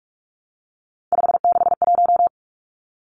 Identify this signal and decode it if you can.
Morse code, heard as a keyed tone, about 44 wpm, 700 Hz. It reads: 561
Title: Numbers - 3-Digit 45wpm